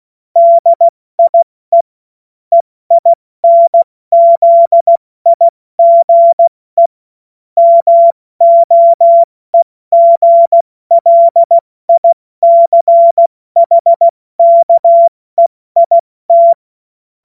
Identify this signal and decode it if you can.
Morse code, heard as a keyed tone, about 16 wpm, 680 Hz. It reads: DIE EINZIGE MOEGLICHKEIT
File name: Morsekurs / Witze / Witz 30. August 2022 70 F80